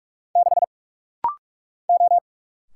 Morse code, heard as a keyed tone, about 44 wpm, 700 Hz. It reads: 6 O
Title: Single Letter-Number Alt - Rapid-Fire 45wpm